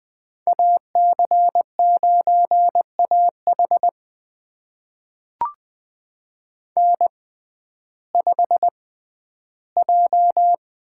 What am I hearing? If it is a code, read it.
Morse code, heard as a keyed tone, about 20 wpm, spaced as for 12 wpm, 700 Hz. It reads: AC9AH N5J
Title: USA Call Signs - 5 Characters - 6x Character Spacing - 20wpm